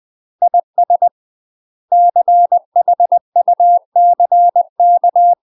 Morse code, heard as a keyed tone, about 20 wpm, 700 Hz. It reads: IS CHUCK
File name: UK MASTER QSOs TRAINING / 20WPM / UK MASTER QSOs 0013